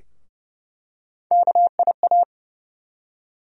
Morse code, heard as a keyed tone, about 31 wpm, 700 Hz. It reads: KIA
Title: USA Call Sign Postfixes - 3 Letters 30wpm